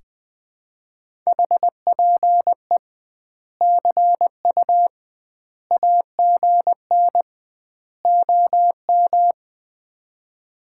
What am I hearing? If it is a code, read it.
Morse code, heard as a keyed tone, about 20 wpm, 700 Hz. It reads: HPE CU AGN OM
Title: Common QSO Phrases - 2x Word-Spacing - 20wpm